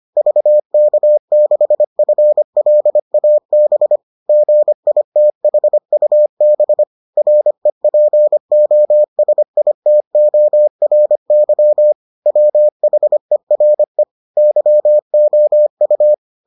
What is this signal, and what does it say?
Morse code, heard as a keyed tone, about 25 wpm, 600 Hz. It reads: VK6FLAB GITHUB REPOSITORY WHERE YOU